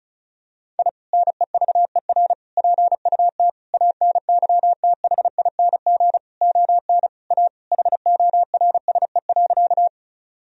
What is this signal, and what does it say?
Morse code, heard as a keyed tone, about 34 wpm, 700 Hz. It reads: I NEVER PUT ANYTHING ON A HORSE.
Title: Sentences from Top 1000 Words - Review Part 1 - 35wpm